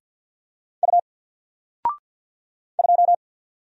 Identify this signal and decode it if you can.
Morse code, heard as a keyed tone, about 50 wpm, 700 Hz. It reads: U 2
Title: Single Character (Letter:Number:Punctuation:Prosign) 50wpm